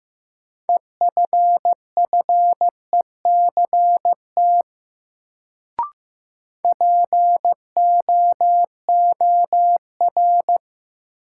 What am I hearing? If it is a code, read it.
Morse code, heard as a keyed tone, about 15 wpm, 700 Hz. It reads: EFFECT POOR
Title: Sentences from Top 1000 Words - New Word Drill - 15wpm